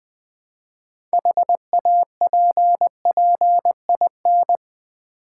Morse code, heard as a keyed tone, about 20 wpm, 700 Hz. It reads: HAPPIN
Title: Build A Word - 3 Letters At A Time - 9-Letter Words - 20wpm